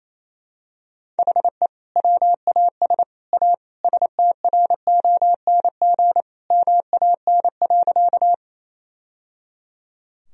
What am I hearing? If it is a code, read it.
Morse code, heard as a keyed tone, about 28 wpm, 700 Hz. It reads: HE WAS A STRONG MAN.
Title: Sentences from Top 1000 Words - Review Part 3 - 28wpm